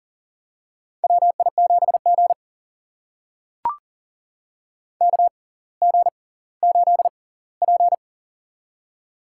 Morse code, heard as a keyed tone, about 40 wpm, spaced as for 20 wpm, 700 Hz. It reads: WI7G KG8P
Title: USA Call Signs - 6x Character Spacing - 40wpm